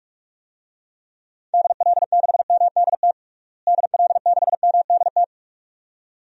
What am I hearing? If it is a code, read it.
Morse code, heard as a keyed tone, about 46 wpm, 700 Hz. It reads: DL6MDT DL6MDT
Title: DXCC-100 Call Signs - 6 Characters - Repeated 2x - 45wpm